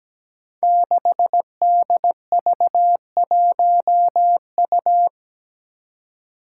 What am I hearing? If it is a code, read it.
Morse code, heard as a keyed tone, about 17 wpm, 700 Hz. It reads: 6DV1U
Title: Five Challenging Letter-Number Combinations 17wpm